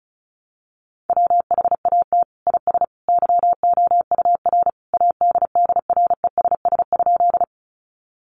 Morse code, heard as a keyed tone, about 34 wpm, 700 Hz. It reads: WHAT IS YOUR ADDRESS?
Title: Sentences from Top 1000 Words 35wpm